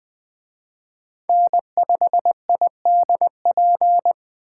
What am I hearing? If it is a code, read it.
Morse code, heard as a keyed tone, about 20 wpm, 700 Hz. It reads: N5IDP